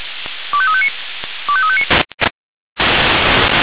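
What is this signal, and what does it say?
Five-tone selective call (ZVEI), heard as quick runs of five tones, tones of about 70 ms. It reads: 25359 25359
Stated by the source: ZVEI five-tone selective call received on SDR.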